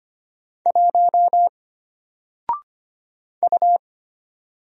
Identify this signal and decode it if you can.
Morse code, heard as a keyed tone, about 25 wpm, 700 Hz. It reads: 1 U
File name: Single-Letter-Number-25wpm